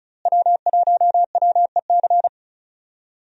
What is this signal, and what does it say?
Morse code, heard as a keyed tone, about 36 wpm, 700 Hz. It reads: W1WEC
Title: USA Call Signs - 5 Characters - Repeated 3x - 35wpm